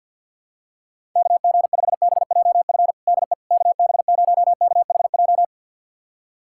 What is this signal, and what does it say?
Morse code, heard as a keyed tone, about 51 wpm, 700 Hz. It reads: KG5BJF DE KB0KHJ
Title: USA Callsign DE Callsign - 6 Characters - 50wpm